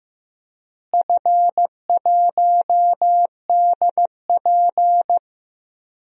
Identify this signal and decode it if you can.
Morse code, heard as a keyed tone, about 15 wpm, 700 Hz. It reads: F1DP
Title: DXCC-100 Call Signs - 4 Characters - Repeated 3x - 15wpm